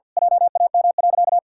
Morse code, heard as a keyed tone, about 51 wpm, 700 Hz. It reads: JAM.
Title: Pangram Challenge v2 - 50wpm